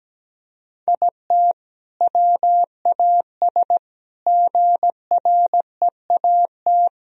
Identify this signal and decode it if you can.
Morse code, heard as a keyed tone, about 17 wpm, 700 Hz. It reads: IT WAS GREAT
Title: Sentences from Top 1000 Words - Review Part 4 - 17wpm